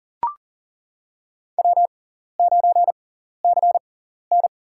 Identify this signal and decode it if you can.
Morse code, heard as a keyed tone, about 37 wpm, spaced as for 14 wpm, 700 Hz. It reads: W9CN
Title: USA Call Signs - 4 Characters - 6x Character Spacing - 40wpm